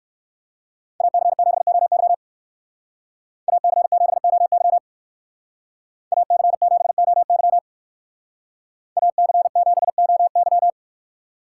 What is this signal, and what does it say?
Morse code, heard as a keyed tone, about 56 wpm, 700 Hz. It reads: AC7OY AC7OY AC7OY AC7OY
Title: Speed Racing - USA 2x2 Call Signs 45wpm